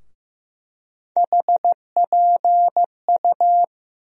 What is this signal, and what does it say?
Morse code, heard as a keyed tone, about 15 wpm, 700 Hz. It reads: HPU
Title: USA Call Sign Postfixes - 3 Letters 15wpm